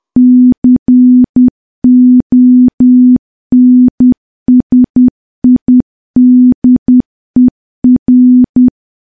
Morse code, heard as a keyed tone, about 10 wpm, 250 Hz. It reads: CONSIDER
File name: morse / morse-training-10WPM